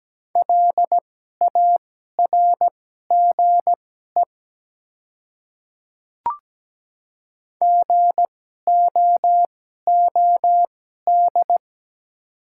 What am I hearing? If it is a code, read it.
Morse code, heard as a keyed tone, about 17 wpm, spaced as for 12 wpm, 700 Hz. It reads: LARGE GOOD
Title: 2x Character Spacing - Top 300 Words - 17wpm